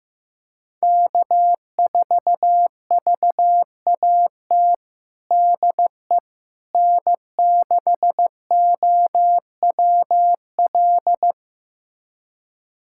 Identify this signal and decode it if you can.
Morse code, heard as a keyed tone, about 15 wpm, 700 Hz. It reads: K4VAT DE N6OWL
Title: USA Callsign DE Callsign - 5 Characters - 15wpm